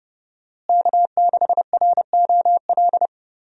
Morse code, heard as a keyed tone, about 31 wpm, 700 Hz. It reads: K6ROL
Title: USA Call Signs - 5 Characters - Repeated 2x - 30wpm